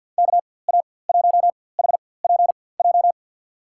Morse code, heard as a keyed tone, about 49 wpm, spaced as for 23 wpm, 700 Hz. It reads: KA1HPJ
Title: USA Call Signs - 6 Characters - 4x Character Spacing - 50wpm